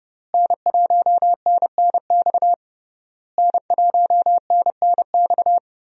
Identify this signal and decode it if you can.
Morse code, heard as a keyed tone, about 30 wpm, 700 Hz. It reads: N1NNX N1NNX
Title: USA Call Signs - 5 Characters - Repeated 3x - 30wpm